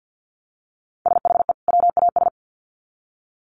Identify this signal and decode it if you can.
Morse code, heard as a keyed tone, about 48 wpm, 700 Hz. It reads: SHE WAS
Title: Sets of 2 Words 50wpm